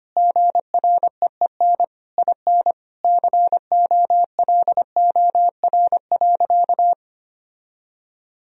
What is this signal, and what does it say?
Morse code, heard as a keyed tone, about 25 wpm, 700 Hz. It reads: GREEN IN COLOR.
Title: Sentences from Top 1000 Words - Review Part 2 - 25wpm